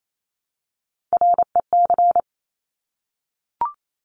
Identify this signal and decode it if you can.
Morse code, heard as a keyed tone, about 28 wpm, 700 Hz. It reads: REC
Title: Top Three Letter Combinations in Words 28wpm